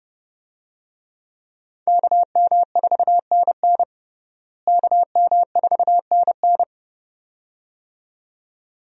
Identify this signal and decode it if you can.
Morse code, heard as a keyed tone, about 32 wpm, 700 Hz. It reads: KM4NN KM4NN
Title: USA Call Signs - 5 Characters - Repeated 2x - 30wpm